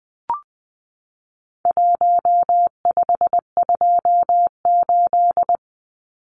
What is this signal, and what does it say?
Morse code, heard as a keyed tone, about 20 wpm, 700 Hz. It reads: 1528
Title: Numbers - Contest Style - 4-Digit 20wpm